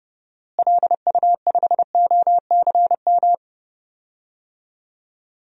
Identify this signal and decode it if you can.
Morse code, heard as a keyed tone, about 30 wpm, 700 Hz. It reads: LU5OCM
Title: DXCC-100 Call Signs - 6 Characters - Repeated 2x - 30wpm